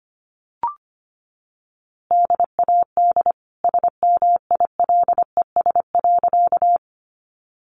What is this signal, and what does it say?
Morse code, heard as a keyed tone, about 25 wpm, 700 Hz. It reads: DAD SMILES.
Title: Easy Phrases 25wpm